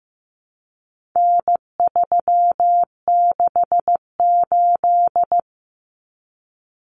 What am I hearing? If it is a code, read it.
Morse code, heard as a keyed tone, about 15 wpm, 700 Hz. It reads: N368
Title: Common Cut Numbers - 4-Digit 15wpm